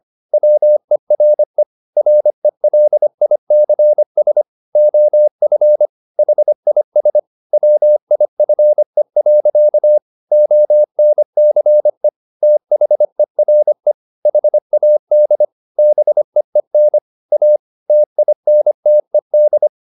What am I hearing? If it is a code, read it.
Morse code, heard as a keyed tone, about 25 wpm, 600 Hz. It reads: WERE RELICS OF HIS WIFE. ONCE THERE HAD BEEN A TINTED
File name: the old man and the sea part 1 25wpm